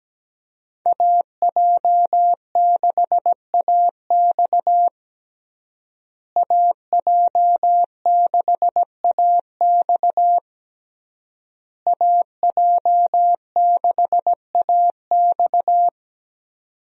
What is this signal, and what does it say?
Morse code, heard as a keyed tone, about 17 wpm, 700 Hz. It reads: AJ6AX AJ6AX AJ6AX
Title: USA Call Signs - 5 Characters - Repeated 3x - 17wpm